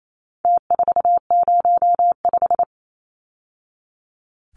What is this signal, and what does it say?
Morse code, heard as a keyed tone, about 28 wpm, 700 Hz. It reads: T405
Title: Numbers - Contest Style - 4-Digit 28wpm